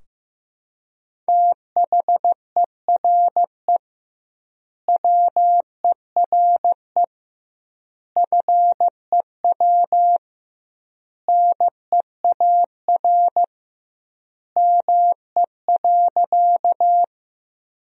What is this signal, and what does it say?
Morse code, heard as a keyed tone, about 15 wpm, 700 Hz. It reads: THERE WERE FEW NEAR ME.
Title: 2x Word-Spacing - Sentences from Top 300 Words - 15wpm